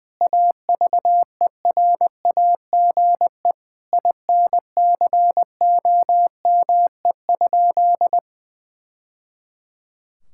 Morse code, heard as a keyed tone, about 21 wpm, 700 Hz. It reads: AVERAGE INCOME?
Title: Sentences from Top 1000 Words - Review Part 6 - 20wpm